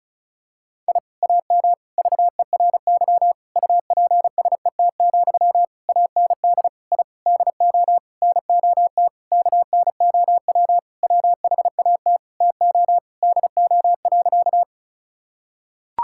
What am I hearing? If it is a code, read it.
Morse code, heard as a keyed tone, about 36 wpm, 700 Hz. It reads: I AM VERY UPSET, AND I DO NOT KNOW WHAT TO DO.
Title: Sentences from Top 1000 Words - Review Part 5 - 35wpm